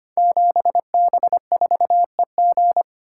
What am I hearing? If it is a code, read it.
Morse code, heard as a keyed tone, about 25 wpm, 690 Hz. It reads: 7B4EG
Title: Code Groups - 5 Letter:Numbers - 25wpm